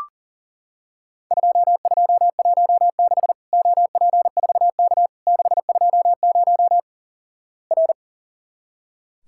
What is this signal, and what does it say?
Morse code, heard as a keyed tone, about 43 wpm, 700 Hz. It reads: 2216 OP4K 620 R
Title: Worked All Europe DX Contest - QTC Exchange - 40wpm